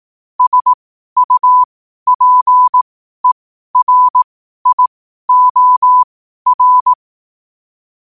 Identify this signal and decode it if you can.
Morse code, heard as a keyed tone, about 18 wpm, spaced as for 12 wpm, 1000 Hz. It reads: SUPERIOR
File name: qotdfiles / 12wpm